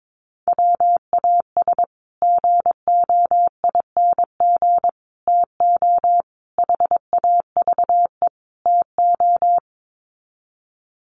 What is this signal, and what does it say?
Morse code, heard as a keyed tone, about 22 wpm, 700 Hz. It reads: WAS GOING TO HAVE TO
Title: Sets of 5 Words Encore 22wpm